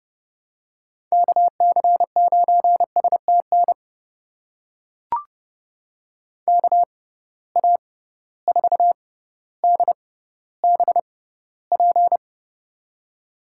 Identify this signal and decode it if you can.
Morse code, heard as a keyed tone, about 32 wpm, spaced as for 16 wpm, 700 Hz. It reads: KC9STN KA4DBP
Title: USA Call Signs - 6x Character Spacing - 30wpm